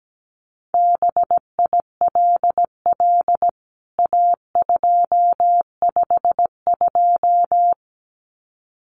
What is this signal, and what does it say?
Morse code, heard as a keyed tone, about 17 wpm, 700 Hz. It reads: BILL A252
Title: CWT Exchange 17wpm